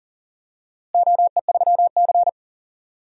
Morse code, heard as a keyed tone, about 40 wpm, 700 Hz. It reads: OE3C
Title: DXCC-100 Call Signs - 4 Characters - Repeated 2x - 40wpm